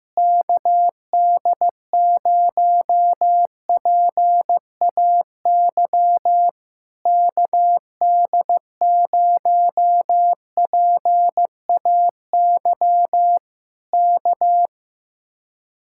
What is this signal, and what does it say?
Morse code, heard as a keyed tone, about 15 wpm, 700 Hz. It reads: KD0PAY KD0PAY K
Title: Speed-Racing Common QSO Phrases Encore - 15wpm